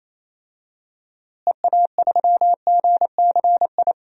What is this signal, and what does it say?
Morse code, heard as a keyed tone, about 27 wpm, 700 Hz. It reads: EA3GCI
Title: DXCC-100 Call Signs - 6 Characters - Repeated 2x - 28wpm